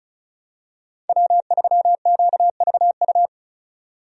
Morse code, heard as a keyed tone, about 36 wpm, 700 Hz. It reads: W3QVU
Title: USA Call Signs - 5 Characters - 35wpm